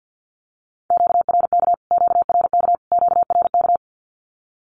Morse code, heard as a keyed tone, about 48 wpm, 700 Hz. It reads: QRK QRK QRK
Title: Q-Codes Repeated 3x 50wpm